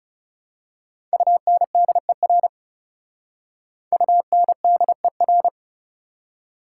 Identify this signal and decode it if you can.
Morse code, heard as a keyed tone, about 34 wpm, 690 Hz. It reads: UNDER UNDER
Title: Speed-Racing Prefixes and Suffixes 30wpm